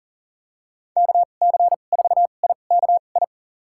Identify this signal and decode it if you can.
Morse code, heard as a keyed tone, about 40 wpm, spaced as for 29 wpm, 700 Hz. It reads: KC4IKI
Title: USA Call Signs - 2x Character Spacing - 40wpm